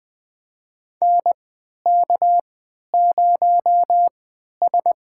Morse code, heard as a keyed tone, about 21 wpm, spaced as for 11 wpm, 700 Hz. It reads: NK0S
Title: USA Call Signs - 4 Characters - 3x Character Spacing - 20wpm